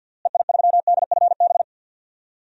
Morse code, heard as a keyed tone, about 51 wpm, 700 Hz. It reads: EI3DFB
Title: DXCC-100 Call Signs - 6 Characters - Repeated 3x - 50wpm